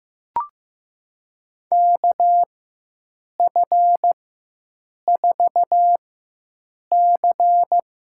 Morse code, heard as a keyed tone, about 15 wpm, spaced as for 7 wpm, 700 Hz. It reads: KF4C
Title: USA Call Signs - 4x Character Spacing - 15wpm